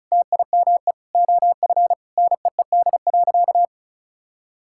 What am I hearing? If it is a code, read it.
Morse code, heard as a keyed tone, about 36 wpm, 700 Hz. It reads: TIME OF NEED.
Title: Sentences from Top 1000 Words - Review Part 5 - 35wpm